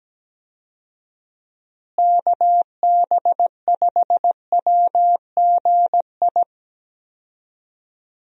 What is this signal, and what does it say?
Morse code, heard as a keyed tone, about 17 wpm, 700 Hz. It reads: KB5WGI
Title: USA Call Signs - 6 Characters - Repeated 3x - 17wpm